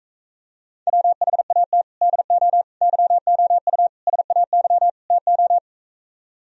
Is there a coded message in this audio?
Yes: Morse code from 0.9 to 5.6 s, heard as a keyed tone, about 45 wpm, 700 Hz. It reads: WHAT DO YOU SAY TO